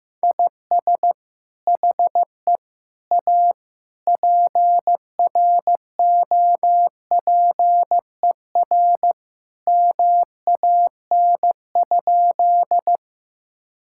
Morse code, heard as a keyed tone, about 15 wpm, 700 Hz. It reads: IS HE A PROPER MAN?
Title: Sentences from Top 1000 Words - Review Part 5 - 15wpm